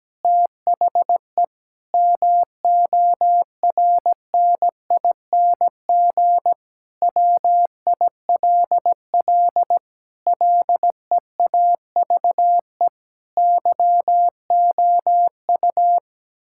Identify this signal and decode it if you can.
Morse code, heard as a keyed tone, about 17 wpm, 700 Hz. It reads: THE MORNING WILL LEAVE YOU
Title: Sentences from Top 1600 Words 17wpm